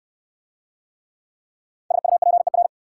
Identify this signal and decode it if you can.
Morse code, heard as a keyed tone, about 61 wpm, 690 Hz. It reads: SUPER